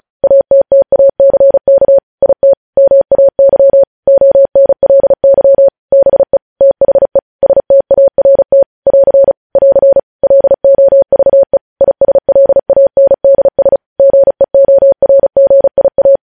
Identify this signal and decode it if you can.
Morse code, heard as a keyed tone, about 36 wpm, 550 Hz. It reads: ATTACK IT MAY ONLY BE THE START <AR> <AR> LOVE ISLANDS GEORGIA
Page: news headlines in morse code at 35 wpm